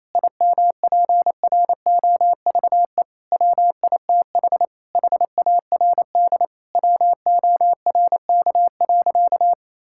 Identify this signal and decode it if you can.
Morse code, heard as a keyed tone, about 27 wpm, 700 Hz. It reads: IMPROVE WITH HARD WORK.
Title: Sentences from Top 1000 Words - Review Part 5 - 28wpm